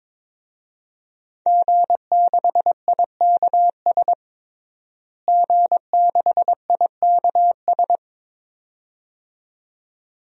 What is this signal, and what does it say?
Morse code, heard as a keyed tone, about 22 wpm, 700 Hz. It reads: G6IKS G6IKS
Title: DXCC-100 Call Signs - 5 Characters - Repeated 2x - 22wpm